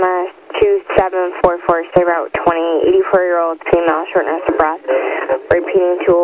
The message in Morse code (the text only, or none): none